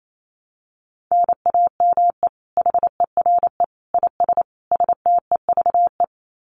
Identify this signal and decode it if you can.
Morse code, heard as a keyed tone, about 28 wpm, 700 Hz. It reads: NAME HERE IS STEVE
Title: Common QSO Phrases 28wpm